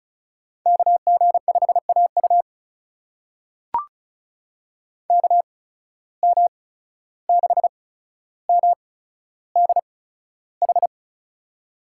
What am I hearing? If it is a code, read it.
Morse code, heard as a keyed tone, about 36 wpm, spaced as for 14 wpm, 700 Hz. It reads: KG5AU KM6MDH
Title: USA Call Signs - 8x Character Spacing - 35wpm